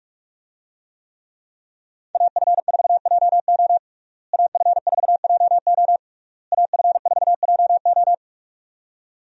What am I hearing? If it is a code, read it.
Morse code, heard as a keyed tone, about 46 wpm, 700 Hz. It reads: AF4JO AF4JO AF4JO
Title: USA Call Signs - 5 Characters - Repeated 3x - 45wpm